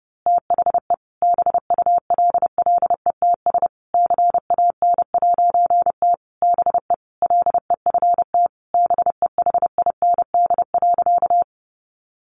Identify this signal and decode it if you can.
Morse code, heard as a keyed tone, about 30 wpm, 700 Hz. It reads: THE BULLETS CAN'T BE LEFT BEHIND.
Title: A Sound Of Thunder 30wpm